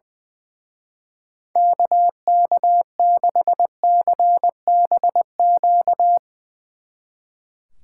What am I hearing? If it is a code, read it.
Morse code, heard as a keyed tone, about 22 wpm, 700 Hz. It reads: KK6CBQ
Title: Speed Racing - USA 2x3 Call Signs 20wpm